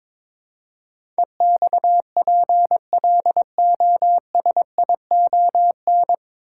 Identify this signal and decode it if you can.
Morse code, heard as a keyed tone, about 22 wpm, 700 Hz. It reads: EXPLOSION
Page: Morse Code Ninja practice 22wpm